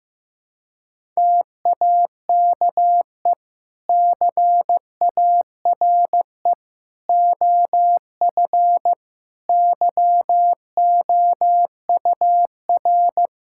Morse code, heard as a keyed tone, about 15 wpm, 700 Hz. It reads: TAKE CARE OF YOUR
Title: Sentences from Top 1000 Words - Review Part 5 - 15wpm